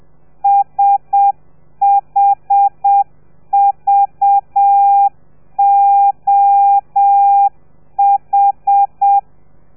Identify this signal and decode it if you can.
Morse code, heard as a keyed tone, about 7 wpm, 800 Hz. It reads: SHVOH
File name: telegrafia / i2tfj / Lez4 30